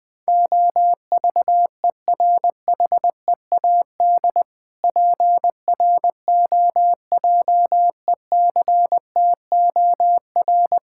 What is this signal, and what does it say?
Morse code, heard as a keyed tone, about 20 wpm, 700 Hz. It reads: OVERHEAD PROJECTOR
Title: Difficult Words - Speed-Racing with Hints - 20wpm